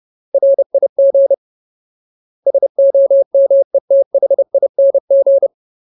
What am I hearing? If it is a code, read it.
Morse code, heard as a keyed tone, about 30 wpm, 550 Hz. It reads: RIG SOMETHING
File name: Ultimate Word List / 30WPM Ultimate